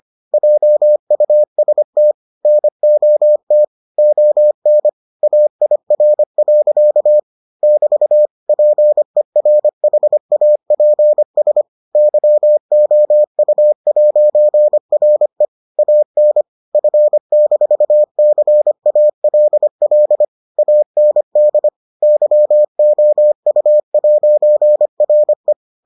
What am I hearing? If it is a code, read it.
Morse code, heard as a keyed tone, about 25 wpm, 600 Hz. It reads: JUST NOT ON AIR. <BT> PERHAPS YOU'RE AN F-CALL AND YOU'RE